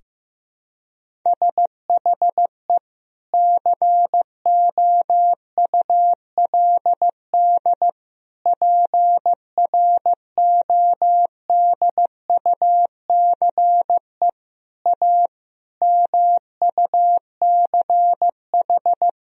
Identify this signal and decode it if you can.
Morse code, heard as a keyed tone, about 15 wpm, 700 Hz. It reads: SHE COULD PRODUCE A MUCH
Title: Sentences from Top 1000 Words - Review Part 3 - 15wpm